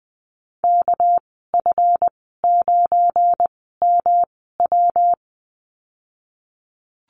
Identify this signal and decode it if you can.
Morse code, heard as a keyed tone, about 20 wpm, spaced as for 15 wpm, 700 Hz. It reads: KF9MW
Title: 213 Lesson 34g - Number 8 - Call Signs 20wpm